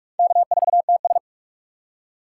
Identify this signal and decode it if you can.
Morse code, heard as a keyed tone, about 46 wpm, 700 Hz. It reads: K4TS
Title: USA Call Signs - 4 Characters - 6x Character Spacing - 45wpm